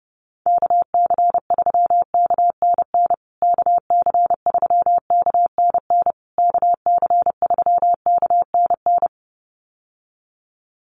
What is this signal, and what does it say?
Morse code, heard as a keyed tone, about 30 wpm, 700 Hz. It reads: KC3KNN KC3KNN KC3KNN
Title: USA Call Signs Repeated 3x 30wpm